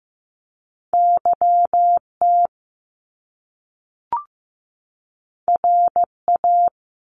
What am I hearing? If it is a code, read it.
Morse code, heard as a keyed tone, about 15 wpm, 700 Hz. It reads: YT RA
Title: Top Two Letter Combinations in Words 15wpm